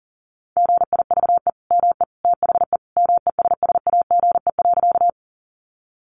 Morse code, heard as a keyed tone, about 40 wpm, 700 Hz. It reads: GIVE ME THE MESSAGE.
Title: Sentences from Top 1100 Words 40wpm